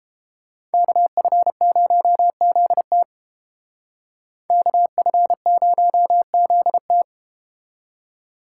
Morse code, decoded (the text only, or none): KF0ZT KF0ZT